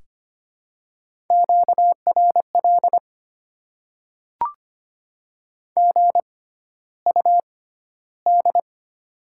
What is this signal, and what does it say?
Morse code, heard as a keyed tone, about 25 wpm, spaced as for 13 wpm, 700 Hz. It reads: QRL GUD
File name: Top 20 QSO Elements - 6x Character Spacing - 25wpm